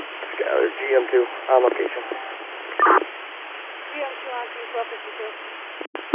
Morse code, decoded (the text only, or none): none